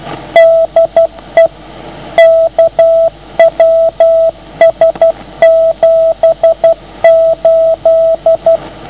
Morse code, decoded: DE KWS78